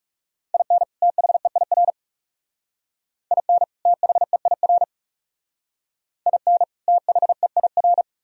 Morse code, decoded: IN THEIR IN THEIR IN THEIR

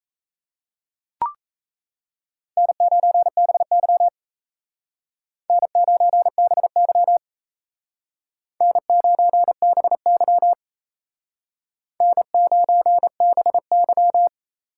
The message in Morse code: N9BY N9BY N9BY N9BY

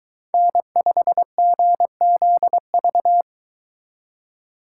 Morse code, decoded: N5GZV

